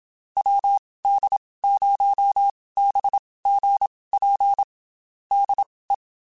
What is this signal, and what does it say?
Morse code, heard as a keyed tone, about 27 wpm, spaced as for 19 wpm, 800 Hz. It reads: WD0BGP DE
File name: QSOs / QSOs-mp3s / WD0BGP 1@25